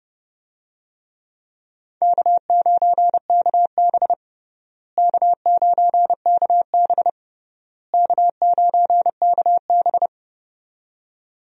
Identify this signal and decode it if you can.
Morse code, heard as a keyed tone, about 31 wpm, 700 Hz. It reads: K9KB K9KB K9KB
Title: USA Call Signs - 4 Characters - Repeated 3x - 30wpm